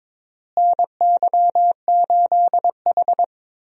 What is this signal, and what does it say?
Morse code, heard as a keyed tone, about 22 wpm, 700 Hz. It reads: NY8H
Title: USA Call Signs - 4 Characters - Repeated 3x - 22wpm